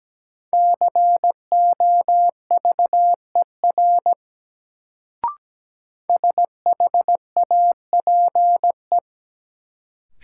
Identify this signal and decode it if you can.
Morse code, heard as a keyed tone, about 17 wpm, 700 Hz. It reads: COVER SHAPE
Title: Sentences from Top 500 Words - New Word Drill - 17wpm